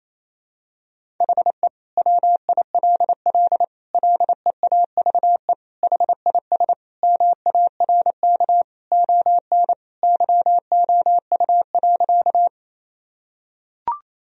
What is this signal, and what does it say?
Morse code, heard as a keyed tone, about 28 wpm, 700 Hz. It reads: HE WILL LEAVE HIS MARK ON YOU.
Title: Sentences from Top 1000 Words - Review Part 1 - 28wpm